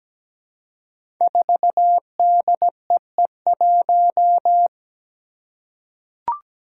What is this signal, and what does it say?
Morse code, heard as a keyed tone, about 17 wpm, 700 Hz. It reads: 4DEE1
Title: Five Challenging Letter-Number Combinations 17wpm